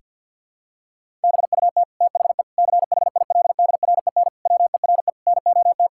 Morse code, heard as a keyed tone, about 52 wpm, 700 Hz. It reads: BUT THE CHILDREN WERE NOT
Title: Sentences from Top 1000 Words - Review Part 6 - 50wpm